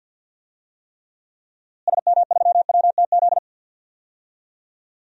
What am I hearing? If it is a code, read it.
Morse code, heard as a keyed tone, about 46 wpm, 700 Hz. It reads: SM3WTZ
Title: DXCC-100 Call Signs - 6 Characters - Repeated 2x - 50wpm